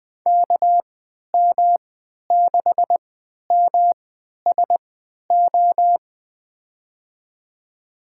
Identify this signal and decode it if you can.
Morse code, heard as a keyed tone, about 20 wpm, spaced as for 11 wpm, 700 Hz. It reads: KM6MSO
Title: USA Call Signs - 6 Characters - 3x Character Spacing - 20wpm